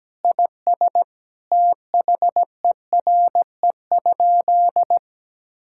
Morse code, heard as a keyed tone, about 17 wpm, 700 Hz. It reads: IS THERE?
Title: Sentences from Top 1600 Words 17wpm